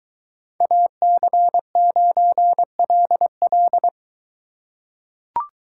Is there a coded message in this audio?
Yes: Morse code from 0.6 to 3.9 s, heard as a keyed tone, about 22 wpm, 700 Hz. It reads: AC9LL